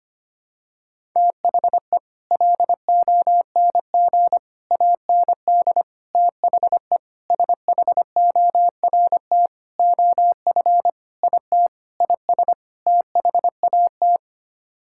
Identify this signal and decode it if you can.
Morse code, heard as a keyed tone, about 25 wpm, 700 Hz. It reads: THE LONG AND THE SHORT OF IT IS THAT
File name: Sets of 9 Words - 25wpm